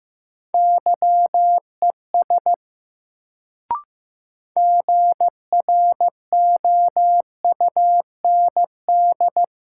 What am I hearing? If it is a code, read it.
Morse code, heard as a keyed tone, about 15 wpm, 700 Hz. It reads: YES GROUND
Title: Sentences from Top 500 Words - New Word Drill - 15wpm